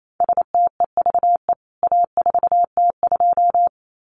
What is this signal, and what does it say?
Morse code, heard as a keyed tone, about 28 wpm, 700 Hz. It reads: STEVE A4T2
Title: CWT Exchange 28wpm